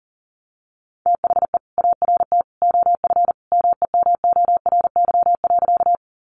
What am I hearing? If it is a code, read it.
Morse code, heard as a keyed tone, about 40 wpm, 700 Hz. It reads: THE ART OF MEMORY.
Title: Sentences from Top 1300 Words 40wpm